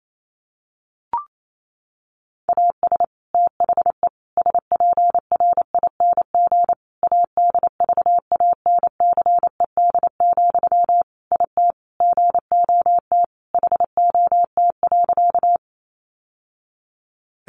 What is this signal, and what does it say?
Morse code, heard as a keyed tone, about 28 wpm, 700 Hz. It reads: AS THE SPRING ADVANCED, IT GOT HOT.
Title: Sentences from Top 1200 Words 28wpm